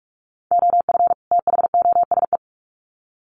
Morse code, heard as a keyed tone, about 44 wpm, 700 Hz. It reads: OF THOSE
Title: Sets of 2 Words 45wpm